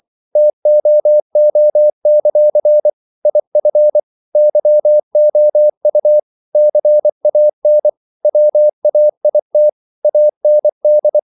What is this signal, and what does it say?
Morse code, heard as a keyed tone, about 24 wpm, 600 Hz. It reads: TOO; IF YOU CAN WAIT AND
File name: CW / Practice / If24